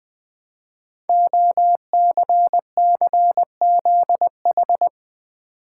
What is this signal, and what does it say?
Morse code, heard as a keyed tone, about 20 wpm, 700 Hz. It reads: OCCZH
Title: Code Groups - 5 Letters - 20wpm